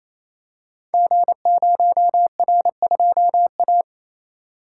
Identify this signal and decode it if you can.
Morse code, heard as a keyed tone, about 27 wpm, 700 Hz. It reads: G0R2A
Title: Code Groups - 5 Letter:Numbers - 28wpm